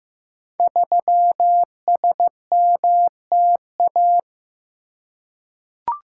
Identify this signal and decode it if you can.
Morse code, heard as a keyed tone, about 15 wpm, 700 Hz. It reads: 3SMTA
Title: Code Groups - 5 Letter:Numbers - 15wpm